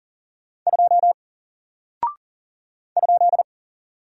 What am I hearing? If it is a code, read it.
Morse code, heard as a keyed tone, about 38 wpm, 700 Hz. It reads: 2 ?
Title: Single Character (Letter:Number:Punctuation:Prosign) Alt 40wpm